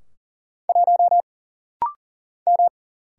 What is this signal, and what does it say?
Morse code, heard as a keyed tone, about 42 wpm, 700 Hz. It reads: 1 M
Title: Single Letter-Number - Rapid-Fire 40wpm